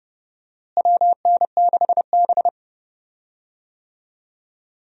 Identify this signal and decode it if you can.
Morse code, heard as a keyed tone, about 32 wpm, 700 Hz. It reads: WN6B
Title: USA Call Signs - 4 Characters - Repeated 2x - 30wpm